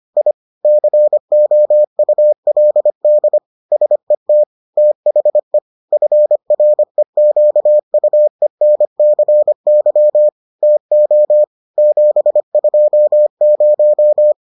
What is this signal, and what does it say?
Morse code, heard as a keyed tone, about 25 wpm, 600 Hz. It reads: I COULD SET THE FREQUENCY TO 720